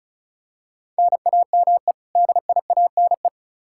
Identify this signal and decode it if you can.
Morse code, heard as a keyed tone, about 36 wpm, 700 Hz. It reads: NAME DIANE
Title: K1USN SST QSO Practice 35wpm